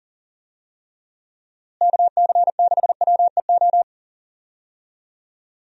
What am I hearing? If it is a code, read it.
Morse code, heard as a keyed tone, about 40 wpm, 700 Hz. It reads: KC6WEO
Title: USA Call Signs - 6 Characters - Repeated 2x - 40wpm